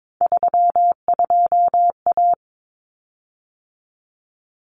Morse code, heard as a keyed tone, about 22 wpm, 700 Hz. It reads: 32A
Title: Common Cut Numbers - 3-Digit 22wpm